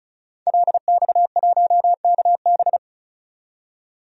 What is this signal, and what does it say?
Morse code, heard as a keyed tone, about 36 wpm, 700 Hz. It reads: LX1KB
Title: DXCC-100 Call Signs - 5 Characters - 35wpm